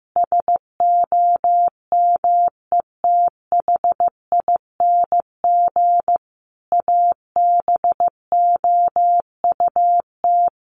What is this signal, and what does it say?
Morse code, heard as a keyed tone, about 15 wpm, 700 Hz. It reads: SOMETHING ABOUT
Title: Sentences from Top 1500 Words 15wpm